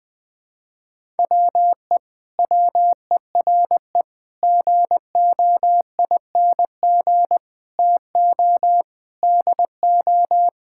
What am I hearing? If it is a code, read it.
Morse code, heard as a keyed tone, about 21 wpm, 700 Hz. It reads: WE WERE GOING TO DO
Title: Sentences with 8-Word Phrases - 20wpm